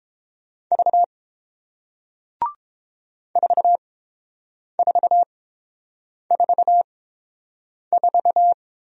Morse code, heard as a keyed tone, about 30 wpm, spaced as for 7 wpm, 690 Hz. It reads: V 4444